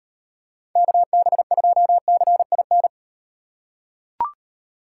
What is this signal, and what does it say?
Morse code, heard as a keyed tone, about 36 wpm, 700 Hz. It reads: KB2CIN